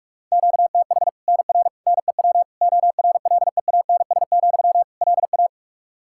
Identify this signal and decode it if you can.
Morse code, heard as a keyed tone, about 43 wpm, 700 Hz. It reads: QTH NR NEW ORLEANS, LA
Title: Practice QSO - Exchange 1 - Beginning - 45wpm